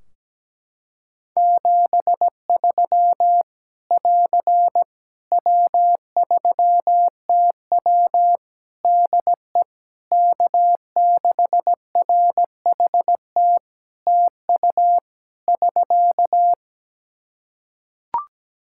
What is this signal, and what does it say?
Morse code, heard as a keyed tone, about 17 wpm, 700 Hz. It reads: 73 <AR> W3TW DE K6RHT TU <SK>
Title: Practice QSO - Exchange 4 - Ending - 17wpm